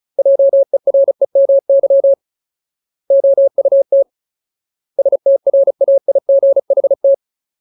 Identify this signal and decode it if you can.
Morse code, heard as a keyed tone, about 34 wpm, 550 Hz. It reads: JEREMY OUT STRAIGHT
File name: Ultimate Word List / 35WPM Ultimate